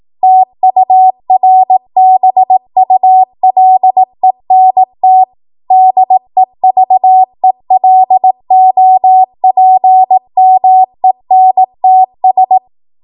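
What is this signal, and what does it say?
Morse code, heard as a keyed tone, about 18 wpm, 750 Hz. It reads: TURBULENT DEVELOPMENTS
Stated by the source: ARRL 18 wpm Morse code practice archive transmission.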